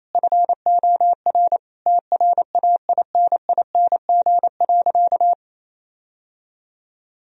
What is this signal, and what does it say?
Morse code, heard as a keyed tone, about 28 wpm, 700 Hz. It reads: FOR TRAINING.
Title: Sentences from Top 1000 Words - Review Part 4 - 28wpm